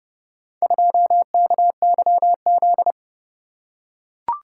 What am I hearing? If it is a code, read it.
Morse code, heard as a keyed tone, about 30 wpm, 700 Hz. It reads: 2KYZ